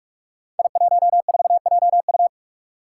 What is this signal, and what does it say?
Morse code, heard as a keyed tone, about 44 wpm, 700 Hz. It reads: I14JU
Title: Five Challenging Letter-Number Combinations 45wpm